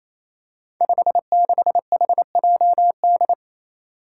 Morse code, heard as a keyed tone, about 28 wpm, 700 Hz. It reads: 56HJD
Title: Five Challenging Letter-Number Combinations 28wpm